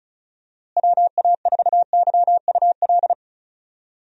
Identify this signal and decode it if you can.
Morse code, heard as a keyed tone, about 34 wpm, 700 Hz. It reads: WA4YUL
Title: USA Call Signs - 6 Characters - Repeated 3x - 35wpm